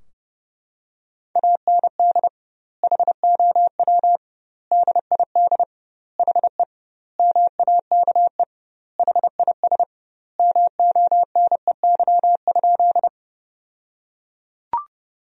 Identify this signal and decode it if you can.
Morse code, heard as a keyed tone, about 32 wpm, 700 Hz. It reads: AND HOW DID HE MAKE HIS MONEY?
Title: 2x Word-Spacing - Sentences from Top 400 Words - 30wpm